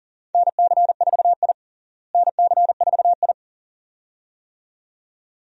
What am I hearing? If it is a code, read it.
Morse code, heard as a keyed tone, about 40 wpm, 700 Hz. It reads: NC4I NC4I
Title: USA Call Signs - 4 Characters - Repeated 2x - 40wpm